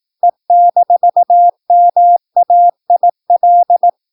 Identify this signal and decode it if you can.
Morse code, heard as a keyed tone, about 18 wpm, 700 Hz. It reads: E-MAIL
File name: PI4AA Uitzending